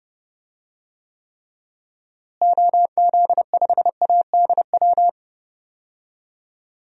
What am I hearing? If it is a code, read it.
Morse code, heard as a keyed tone, about 30 wpm, 700 Hz. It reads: OZ5ADW
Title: DXCC-100 Call Signs - 6 Characters - Repeated 3x - 30wpm